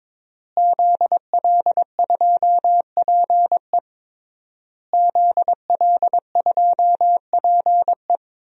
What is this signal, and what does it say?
Morse code, heard as a keyed tone, about 22 wpm, 700 Hz. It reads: ZL2PE ZL2PE
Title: DXCC-100 Call Signs - 5 Characters - Repeated 3x - 22wpm